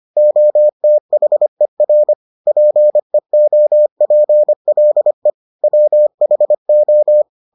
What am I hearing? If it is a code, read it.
Morse code, heard as a keyed tone, about 25 wpm, 600 Hz. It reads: OTHER PEOPLE WHO